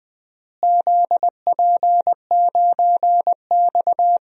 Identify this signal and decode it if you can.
Morse code, heard as a keyed tone, about 20 wpm, 700 Hz. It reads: ZP9X
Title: DXCC-100 Call Signs - 4 Characters - Repeated 2x - 20wpm